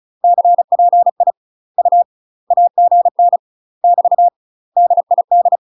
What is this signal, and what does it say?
Morse code, heard as a keyed tone, about 35 wpm, 700 Hz. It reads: CPI U AGN <BT> DID
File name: UK MASTER QSOs TRAINING / 35WPM / UK MASTER QSOs 0035